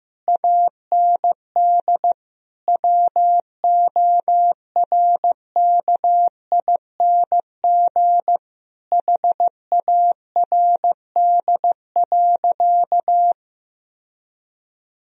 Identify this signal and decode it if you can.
Morse code, heard as a keyed tone, about 15 wpm, 700 Hz. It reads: AND WORKING HARD.
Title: Sentences from Top 1000 Words - Review Part 4 - 15wpm